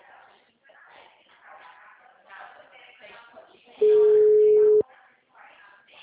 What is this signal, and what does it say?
Ringback tone (Europe, 425 Hz), with speech.